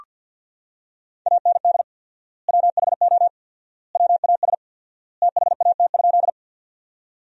Morse code, heard as a keyed tone, about 47 wpm, 700 Hz. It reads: AND WHO WAS THAT?
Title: 4x Word-Spacing - Sentences from Top 100 Words - 50wpm